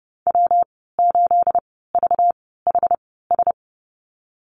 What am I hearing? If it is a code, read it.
Morse code, heard as a keyed tone, about 30 wpm, spaced as for 17 wpm, 700 Hz. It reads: W8VHS
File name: Lesson 37g - Letter x - Call Signs 30wpm